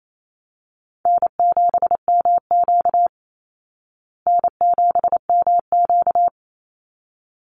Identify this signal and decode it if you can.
Morse code, heard as a keyed tone, about 28 wpm, 700 Hz. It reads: N7MQ N7MQ
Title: Easy Call Signs Repeated 2x - 28wpm